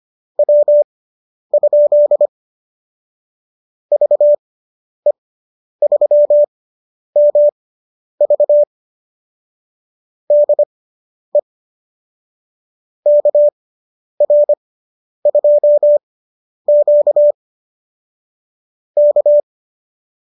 Morse code, heard as a keyed tone, about 25 wpm, spaced as for 10 wpm, 590 Hz. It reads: W? VE3MV DE KR2Q K